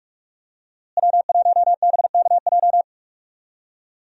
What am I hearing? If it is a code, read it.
Morse code, heard as a keyed tone, about 46 wpm, 700 Hz. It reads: W1BKJ
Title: USA Call Signs - 5 Characters - Repeated 2x - 45wpm